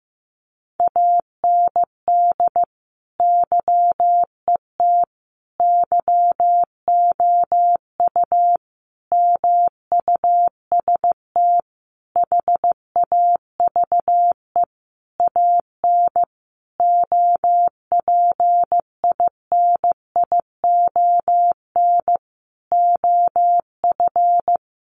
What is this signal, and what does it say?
Morse code, heard as a keyed tone, about 15 wpm, 700 Hz. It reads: AND YET YOU MUST HAVE AN OPINION OF
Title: Sentences from Top 1300 Words 15wpm